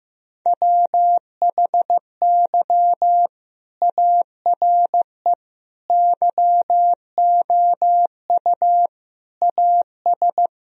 Morse code, heard as a keyed tone, about 15 wpm, 700 Hz. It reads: WHY ARE YOU AS
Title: Sentences from Top 1000 Words - Review Part 3 - 15wpm